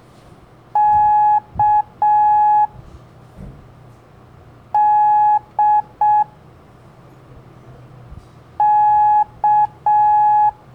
Morse code, heard as a keyed tone, about 6 wpm, 840 Hz. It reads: K D K
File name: blog / LETRAS COM SOM SEMELHANTE MORSE